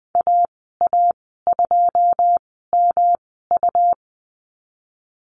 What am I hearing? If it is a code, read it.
Morse code, heard as a keyed tone, about 20 wpm, spaced as for 14 wpm, 700 Hz. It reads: AA2MU
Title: 161 Lesson 26g - Letter g - Call Signs 20wpm